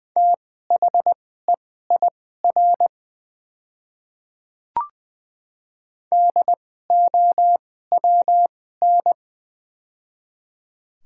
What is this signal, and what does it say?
Morse code, heard as a keyed tone, about 21 wpm, spaced as for 15 wpm, 700 Hz. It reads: THEIR DOWN
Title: 2x Character Spacing - Top 100 Words - 20wpm